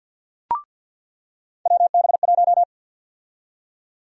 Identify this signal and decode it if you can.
Morse code, heard as a keyed tone, about 49 wpm, 700 Hz. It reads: WB1
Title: USA Call Sign Prefixes - 50wpm